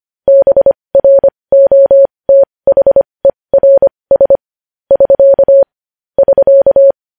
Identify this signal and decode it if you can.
Morse code, heard as a keyed tone, about 25 wpm, spaced as for 20 wpm, 550 Hz. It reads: BROTHERS <SK> <SK>
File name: podcast / News headlines-1x-25x20